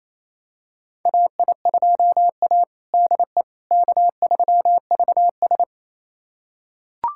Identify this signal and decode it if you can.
Morse code, heard as a keyed tone, about 28 wpm, 700 Hz. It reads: AI2A DE K3VS
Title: USA Callsign DE Callsign - 4 Characters - 28wpm